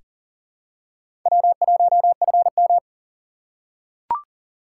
Morse code, heard as a keyed tone, about 40 wpm, 700 Hz. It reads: W1FM
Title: USA Call Signs - 4 Characters - 4x Character Spacing - 40wpm